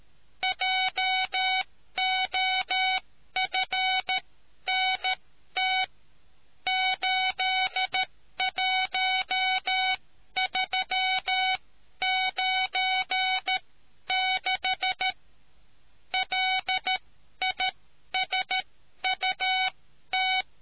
Morse code, heard as a keyed tone, about 13 wpm, spaced as for 11 wpm, 750 Hz. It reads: JOFNT 81396 LISUT